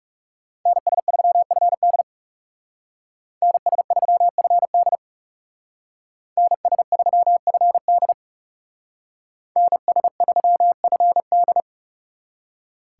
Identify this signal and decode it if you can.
Morse code, heard as a keyed tone, about 36 wpm, 700 Hz. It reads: NS3FD NS3FD NS3FD NS3FD
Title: Speed Racing - USA 2x2 Call Signs 30wpm